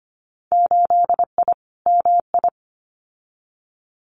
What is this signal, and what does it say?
Morse code, heard as a keyed tone, about 25 wpm, 700 Hz. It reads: 8I MI
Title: Winter Field Day Exchanges Encore 25wpm